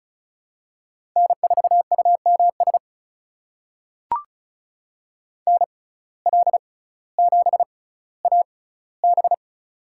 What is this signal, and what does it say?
Morse code, heard as a keyed tone, about 34 wpm, spaced as for 18 wpm, 700 Hz. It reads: N4UMS NL7AB
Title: USA Call Signs - 5 Characters - 6x Character Spacing - 35wpm